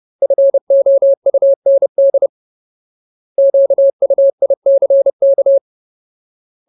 Morse code, heard as a keyed tone, about 30 wpm, 550 Hz. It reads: FOUND QUICK
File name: Ultimate Word List / 30WPM Ultimate